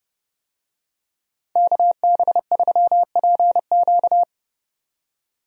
Morse code, heard as a keyed tone, about 30 wpm, 700 Hz. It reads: KB3PQ
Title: USA Call Signs - 5 Characters - 30wpm